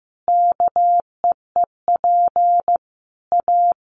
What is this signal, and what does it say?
Morse code, heard as a keyed tone, about 15 wpm, 700 Hz. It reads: KEEP A
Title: Sentences from Top 1300 Words 15wpm